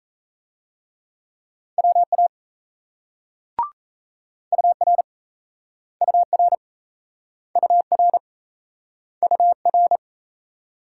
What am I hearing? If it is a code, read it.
Morse code, heard as a keyed tone, about 37 wpm, 700 Hz. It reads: WA UR UR UR UR